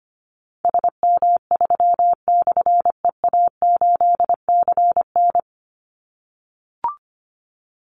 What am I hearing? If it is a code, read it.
Morse code, heard as a keyed tone, about 25 wpm, 700 Hz. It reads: SM3/EA8CN
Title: Call Signs - 25wpm